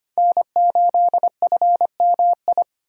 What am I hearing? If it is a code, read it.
Morse code, heard as a keyed tone, about 25 wpm, 700 Hz. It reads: N8FMI
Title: USA Call Signs - 5 Characters - 25wpm